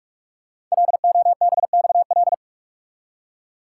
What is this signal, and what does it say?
Morse code, heard as a keyed tone, about 44 wpm, 690 Hz. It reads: LOBXL
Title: Code Groups - 5 Letters - 45wpm